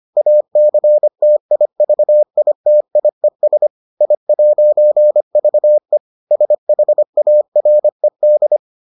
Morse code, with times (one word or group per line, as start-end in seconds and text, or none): 0.2-3.7 ACTIVITIES
4.0-6.0 I'VE
6.3-8.6 SHARED